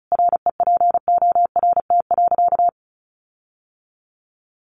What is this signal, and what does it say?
Morse code, heard as a keyed tone, about 34 wpm, 700 Hz. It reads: REPORT.
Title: Sentences from Top 1300 Words 35wpm